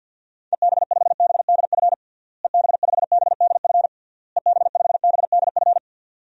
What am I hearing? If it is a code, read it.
Morse code, heard as a keyed tone, about 52 wpm, 700 Hz. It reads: EB5BDF EB5BDF EB5BDF
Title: DXCC-100 Call Signs - 6 Characters - Repeated 3x - 50wpm